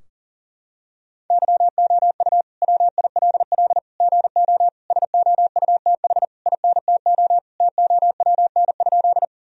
Morse code, heard as a keyed tone, about 40 wpm, 700 Hz. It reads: YOU WILL GO SOUTH INTO TOWN?
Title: Sentences from Top 1000 Words - Review Part 2 - 40wpm